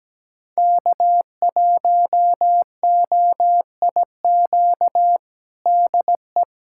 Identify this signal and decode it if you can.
Morse code, heard as a keyed tone, about 17 wpm, 700 Hz. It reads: K1OIQ DE
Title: USA Callsign DE Callsign - 5 Characters - 17wpm